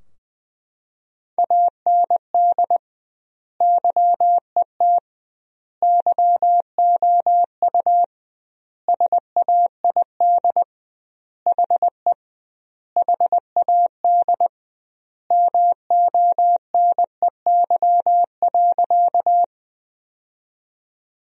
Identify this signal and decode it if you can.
Morse code, heard as a keyed tone, about 20 wpm, 700 Hz. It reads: AND YET YOU SAID HE HAD MONEY.
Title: 2x Word-Spacing - Sentences from Top 500 Words - 20wpm